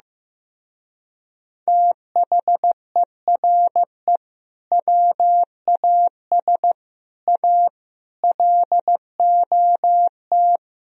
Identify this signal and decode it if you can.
Morse code, heard as a keyed tone, about 15 wpm, 700 Hz. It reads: THERE WAS A LOT
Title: Speed-Racing - Sets of 5 Words 15wpm